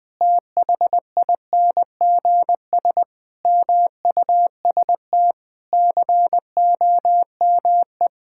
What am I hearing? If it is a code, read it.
Morse code, heard as a keyed tone, about 21 wpm, 700 Hz. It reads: THINGS MUST COME
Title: Sets of 8 Words - 20wpm